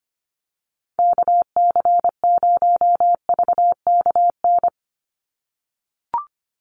Morse code, heard as a keyed tone, about 25 wpm, 700 Hz. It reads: KC0VKN
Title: Call Signs - 25wpm